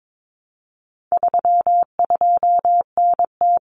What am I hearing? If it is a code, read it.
Morse code, heard as a keyed tone, about 22 wpm, 700 Hz. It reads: 32NT
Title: Common Cut Numbers - 4-Digit 22wpm